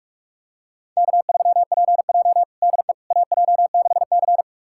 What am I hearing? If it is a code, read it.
Morse code, heard as a keyed tone, about 46 wpm, 700 Hz. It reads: K3PJ DE AJ6C
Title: USA Callsign DE Callsign - 4 Characters - 45wpm